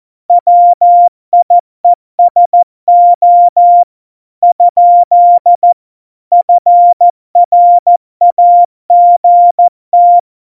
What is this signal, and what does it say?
Morse code, heard as a keyed tone, about 14 wpm, 700 Hz. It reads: WIESO ? FRAGT
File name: Morsekurs / Witze / MP3, Witz 10 April 2018, 70 F80